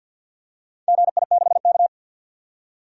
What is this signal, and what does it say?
Morse code, heard as a keyed tone, about 52 wpm, 700 Hz. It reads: GI6K
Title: DXCC-100 Call Signs - 4 Characters - 50wpm